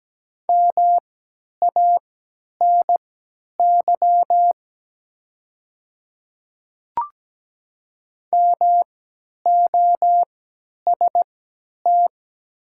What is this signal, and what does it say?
Morse code, heard as a keyed tone, about 17 wpm, spaced as for 10 wpm, 700 Hz. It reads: MANY MOST
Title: 3x Character Spacing - Top 100 Words - 17wpm